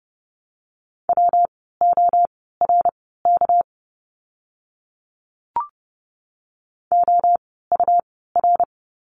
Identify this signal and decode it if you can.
Morse code, heard as a keyed tone, about 30 wpm, spaced as for 17 wpm, 700 Hz. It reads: WORK OUR
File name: course / 30:10wpm / 237 Lesson 38f - Letter q - Words 30wpm